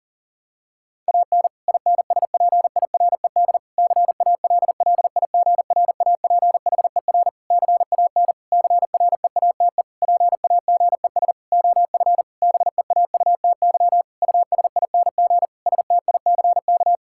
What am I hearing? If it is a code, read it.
Morse code, heard as a keyed tone, about 40 wpm, 700 Hz. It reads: AN INSPIRED CALLIGRAPHER CAN CREATE PAGES OF BEAUTY USING STICK
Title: Pangram Challenge v2 - 40wpm